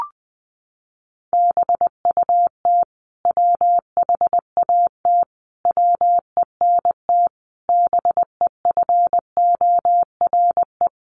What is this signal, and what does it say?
Morse code, heard as a keyed tone, about 20 wpm, 700 Hz. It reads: BUT WHAT WENT BEFORE
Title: Sentences from Top 1400 Words 20wpm